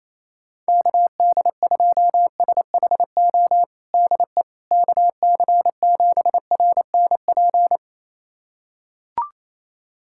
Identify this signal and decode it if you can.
Morse code, heard as a keyed tone, about 28 wpm, 700 Hz. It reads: KD2SHO DE KC7RNP
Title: USA Callsign DE Callsign - 6 Characters - 28wpm